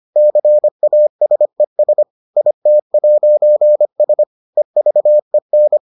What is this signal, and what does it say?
Morse code, heard as a keyed tone, about 25 wpm, 600 Hz. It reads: CASES IT'S EVEN